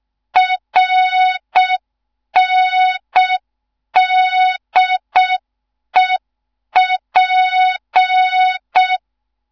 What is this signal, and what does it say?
Morse code, heard as a keyed tone, about 6 wpm, 3700 Hz. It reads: RNDEP
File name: Morze-HA2MN / hang / 15 GYAKORLAT